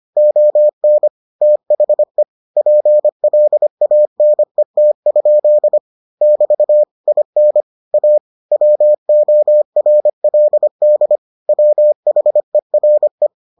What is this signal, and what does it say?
Morse code, heard as a keyed tone, about 25 wpm, 600 Hz. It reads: ON THE PLANET? <BT> IN A WORLD WHERE